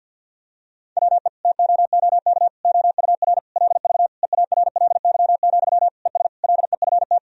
Morse code, heard as a keyed tone, about 50 wpm, 700 Hz. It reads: WE TOOK OUR LV EARLY, ES LEFT
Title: Sentences from Top 400 Words and Abbreviations 50wpm